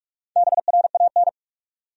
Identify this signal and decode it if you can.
Morse code, heard as a keyed tone, about 48 wpm, 700 Hz. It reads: BRAN
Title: USA Common Names - Max 5 Letters - 45wpm